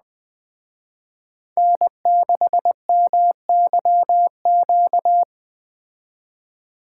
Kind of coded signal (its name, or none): Morse code